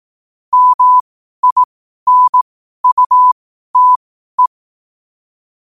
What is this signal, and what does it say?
Morse code, heard as a keyed tone, about 18 wpm, spaced as for 12 wpm, 1000 Hz. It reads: MINUTE
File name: qotdfiles / 12wpm